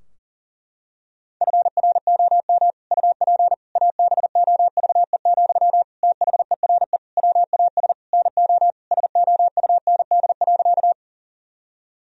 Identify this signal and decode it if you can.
Morse code, heard as a keyed tone, about 40 wpm, 700 Hz. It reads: FROM UP ABOVE, THERE WAS NO SOUND.
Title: Sentences from Top 1000 Words - Review Part 2 - 40wpm